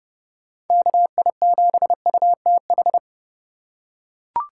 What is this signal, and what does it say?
Morse code, heard as a keyed tone, about 30 wpm, 700 Hz. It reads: KI7UTH
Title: USA Call Signs - 8x Character Spacing - 30wpm